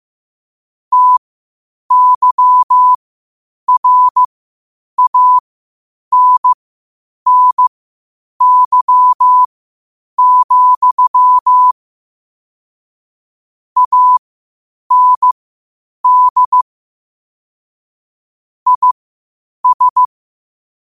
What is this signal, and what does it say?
Morse code, heard as a keyed tone, about 15 wpm, spaced as for 8 wpm, 1000 Hz. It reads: TYRANNY, AND IS
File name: qotdfiles / 8wpm